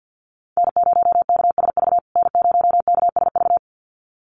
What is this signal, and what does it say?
Morse code, heard as a keyed tone, about 50 wpm, 700 Hz. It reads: N0KSV N0KSV
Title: USA Call Signs Repeated 2x 50wpm